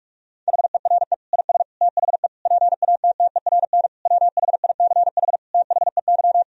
Morse code, heard as a keyed tone, about 46 wpm, 700 Hz. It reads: HERE IS THE PATTERN WHICH THEY
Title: Sentences from Top 1000 Words - Review Part 2 - 45wpm